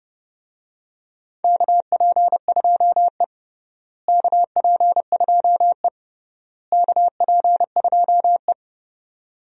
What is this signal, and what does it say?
Morse code, heard as a keyed tone, about 30 wpm, 700 Hz. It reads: KP2E KP2E KP2E
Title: DXCC-100 Call Signs - 4 Characters - Repeated 3x - 30wpm